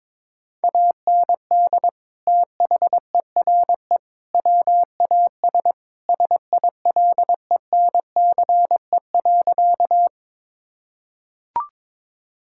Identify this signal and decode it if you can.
Morse code, heard as a keyed tone, about 22 wpm, 700 Hz. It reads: AND THERE WAS SILENCE.
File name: practice / 22wpm / Sentences with 3-Word Phrases - 22wpm